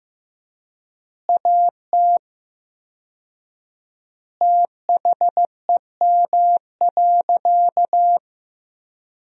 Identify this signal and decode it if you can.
Morse code, heard as a keyed tone, about 15 wpm, 700 Hz. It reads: AT THEM.
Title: 4x Word-Spacing - Sentences from Top 100 Words - 15wpm